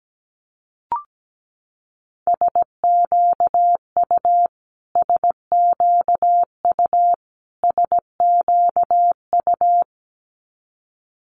Morse code, heard as a keyed tone, about 17 wpm, 700 Hz. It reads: SQU SQU SQU
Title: Multi-letter Phonemes Repeated 3x 17wpm